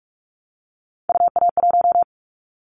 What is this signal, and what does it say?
Morse code, heard as a keyed tone, about 44 wpm, 700 Hz. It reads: UA1
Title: Callsign Prefixes for Top 25 DXCC - Repeated 3x - 45wpm